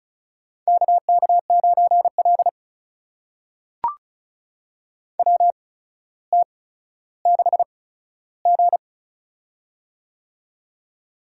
Morse code, heard as a keyed tone, about 34 wpm, spaced as for 15 wpm, 700 Hz. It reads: KK9L WT6G
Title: USA Call Signs - 4 Characters - 8x Character Spacing - 35wpm